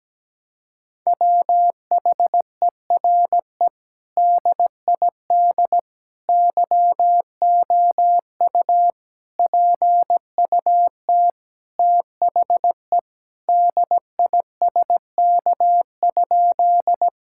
Morse code, heard as a keyed tone, about 17 wpm, 700 Hz. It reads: WHERE DID YOU PUT THE DISK?
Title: Sentences from Top 1000 Words - Review Part 6 - 17wpm